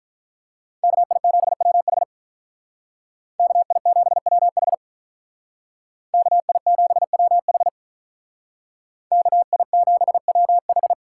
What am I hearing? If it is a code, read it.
Morse code, heard as a keyed tone, about 43 wpm, 700 Hz. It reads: KI7WH KI7WH KI7WH KI7WH
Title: Speed Racing - USA 2x2 Call Signs 35wpm